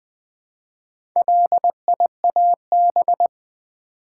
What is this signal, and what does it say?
Morse code, heard as a keyed tone, about 20 wpm, 700 Hz. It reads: LIAB
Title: Build A Word - 2 Letters At A Time - 6-Letter Words - 20wpm